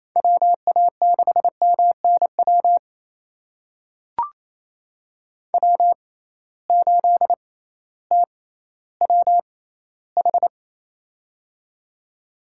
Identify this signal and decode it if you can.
Morse code, heard as a keyed tone, about 28 wpm, spaced as for 15 wpm, 700 Hz. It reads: WA6MNW W8TWH
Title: USA Call Signs - 6x Character Spacing - 28wpm